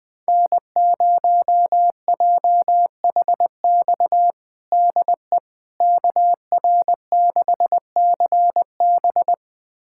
Morse code, heard as a keyed tone, about 20 wpm, 700 Hz. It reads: N0JHX DE KR6CB
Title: USA Callsign DE Callsign - 5 Characters - 20wpm